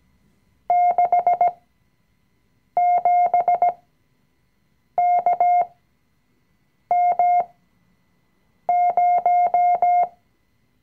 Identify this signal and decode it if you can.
Morse code, heard as a keyed tone, about 17 wpm, spaced as for 6 wpm, 710 Hz. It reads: 67KM0
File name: CW-Online-Kurs / Anfänger / Lektion 020